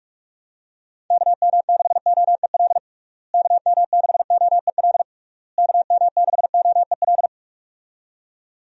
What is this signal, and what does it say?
Morse code, heard as a keyed tone, about 42 wpm, 700 Hz. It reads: KM6OEL KM6OEL KM6OEL
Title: USA Call Signs - 6 Characters - Repeated 3x - 45wpm